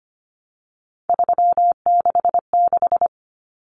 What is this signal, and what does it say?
Morse code, heard as a keyed tone, about 25 wpm, 700 Hz. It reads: 366
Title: Numbers - 3-Digit 25wpm